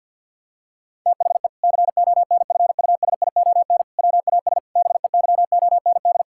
Morse code, heard as a keyed tone, about 50 wpm, 700 Hz. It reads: THE CONFUSION WAS BEYOND
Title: Sentences from Top 1600 Words 50wpm